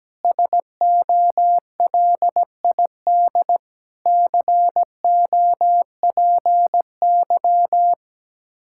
Morse code, heard as a keyed tone, about 17 wpm, 700 Hz. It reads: SOLID COPY
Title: Practice QSO - Exchange 3 - More Info - 17wpm